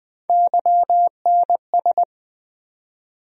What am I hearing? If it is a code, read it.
Morse code, heard as a keyed tone, about 20 wpm, 700 Hz. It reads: YNS
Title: USA Call Sign Postfixes - 3 Letters 20wpm